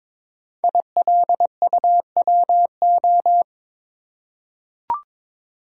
Morse code, heard as a keyed tone, about 22 wpm, 700 Hz. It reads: ILUWO
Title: Code Groups - 5 Letters - 22wpm